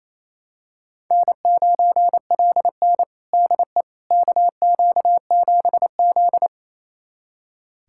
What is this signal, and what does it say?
Morse code, heard as a keyed tone, about 28 wpm, 700 Hz. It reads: N9LN DE KQ7Z
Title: USA Callsign DE Callsign - 4 Characters - 28wpm